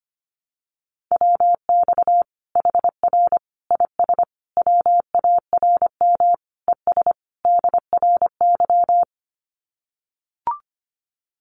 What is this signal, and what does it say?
Morse code, heard as a keyed tone, about 25 wpm, 700 Hz. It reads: WX HR IS WARM ES DRY
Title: Common QSO Phrases 25wpm